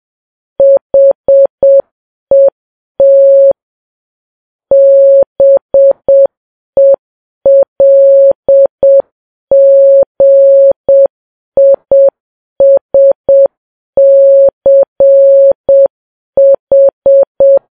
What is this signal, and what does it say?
Morse code, heard as a keyed tone, about 7 wpm, 550 Hz. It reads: HET BELGISCH
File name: Files Web tls / Morse on9fv / morse7kps